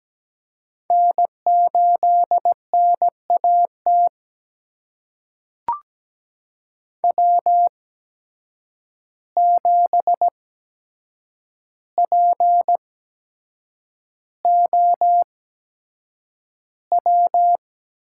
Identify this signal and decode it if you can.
Morse code, heard as a keyed tone, about 17 wpm, spaced as for 7 wpm, 700 Hz. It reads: N8NAT W7POW
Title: USA Call Signs - 8x Character Spacing - 17wpm